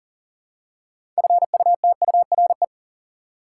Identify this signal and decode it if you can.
Morse code, heard as a keyed tone, about 40 wpm, 700 Hz. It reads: FUTURE
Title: Six Letter Words - 40wpm